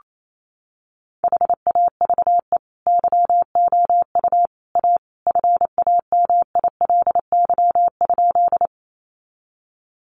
Morse code, heard as a keyed tone, about 28 wpm, 700 Hz. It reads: HAVE YOU A FAMILY?
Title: Sentences from Top 400 Words 28wpm